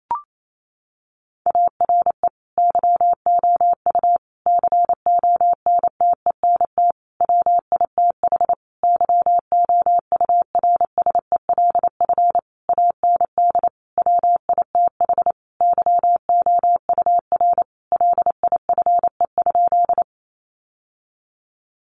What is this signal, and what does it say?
Morse code, heard as a keyed tone, about 28 wpm, 700 Hz. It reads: ARE YOU CONTENT WITH YOURSELF AND WITH YOUR LIFE?
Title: Sentences from Top 900 Words 28wpm